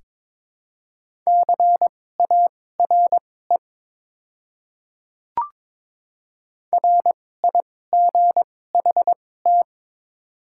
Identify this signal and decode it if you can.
Morse code, heard as a keyed tone, about 22 wpm, spaced as for 16 wpm, 700 Hz. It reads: CARE RIGHT
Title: 2x Character Spacing - Top 300 Words - 22wpm